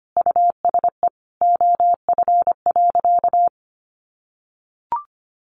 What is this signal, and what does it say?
Morse code, heard as a keyed tone, about 25 wpm, 700 Hz. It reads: USE OF.
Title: Sentences from Top 1200 Words 25wpm